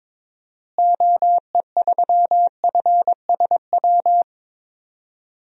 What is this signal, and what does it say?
Morse code, heard as a keyed tone, about 22 wpm, 700 Hz. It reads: OE3FSW
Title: DXCC-100 Call Signs - 6 Characters - Repeated 3x - 22wpm